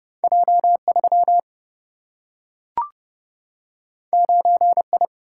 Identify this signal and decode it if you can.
Morse code, heard as a keyed tone, about 30 wpm, 700 Hz. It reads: J3 9I
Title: Single-Letter Single-Number 30wpm